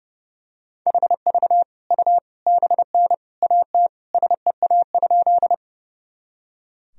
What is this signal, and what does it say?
Morse code, heard as a keyed tone, about 30 wpm, 700 Hz. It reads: HV U BN AT SEA?
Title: Sentences from Top 300 Words and Abbreviations 30wpm